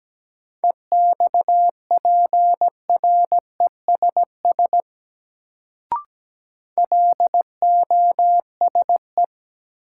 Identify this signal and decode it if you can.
Morse code, heard as a keyed tone, about 17 wpm, 700 Hz. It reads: EXPRESS LOSE
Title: Sentences from Top 1100 Words - New Word Drill - 17wpm